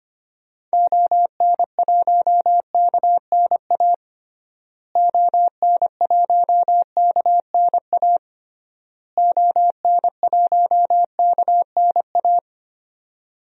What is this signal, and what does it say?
Morse code, heard as a keyed tone, about 25 wpm, 700 Hz. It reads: ON1KNA ON1KNA ON1KNA
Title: DXCC-100 Call Signs - 6 Characters - Repeated 3x - 25wpm